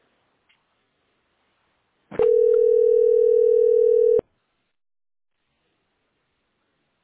Ringback tone (North America), 440 + 480 Hz.